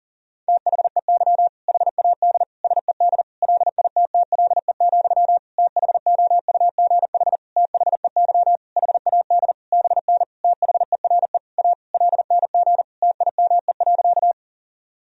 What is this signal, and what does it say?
Morse code, heard as a keyed tone, about 40 wpm, 700 Hz. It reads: THEY HAD SED LITTLE, THOUGH THEY HAD BN THERE A LNG TIME.
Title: Sentences from Top 400 Words and Abbreviations 40wpm